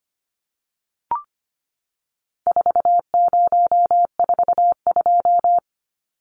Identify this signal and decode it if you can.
Morse code, heard as a keyed tone, about 25 wpm, 700 Hz. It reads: 4042
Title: Numbers - 4-Digit 25wpm